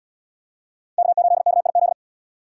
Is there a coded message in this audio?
Yes: Morse code from 1.0 to 1.9 s, heard as a keyed tone, about 76 wpm, 700 Hz. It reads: K8KIZ